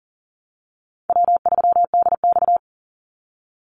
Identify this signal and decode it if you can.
Morse code, heard as a keyed tone, about 40 wpm, 700 Hz. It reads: W3DX
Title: NA SOTA Activator Callsigns 40wpm